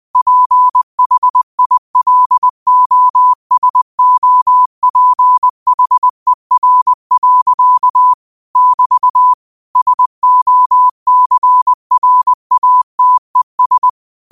Morse code, heard as a keyed tone, about 20 wpm, 1000 Hz. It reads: PHILOSOPHER. <BT> SOCRATES